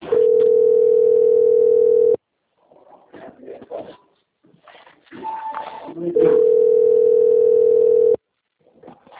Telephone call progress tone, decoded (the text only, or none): ringback tone